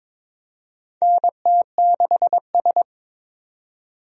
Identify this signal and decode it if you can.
Morse code, heard as a keyed tone, about 22 wpm, 700 Hz. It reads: NT6S
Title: USA Call Signs - 4 Characters - 22wpm